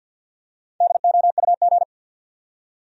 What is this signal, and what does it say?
Morse code, heard as a keyed tone, about 52 wpm, 700 Hz. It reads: DOUG
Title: USA Common Names - 50wpm